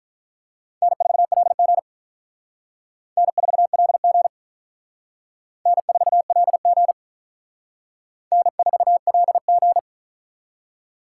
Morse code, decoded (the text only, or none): N4LG N4LG N4LG N4LG